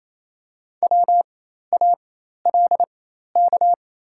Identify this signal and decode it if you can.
Morse code, heard as a keyed tone, about 28 wpm, spaced as for 13 wpm, 700 Hz. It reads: WALK
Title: 4x Character Spacing - Top 500 Words - 28wpm